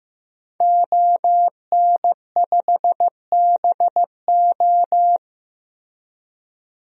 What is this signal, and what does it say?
Morse code, heard as a keyed tone, about 15 wpm, 700 Hz. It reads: ON5BO
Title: DXCC-100 Call Signs - 5 Characters - Repeated 2x - 15wpm